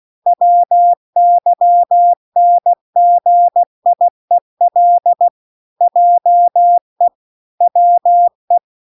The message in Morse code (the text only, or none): WYNGIEL JE WE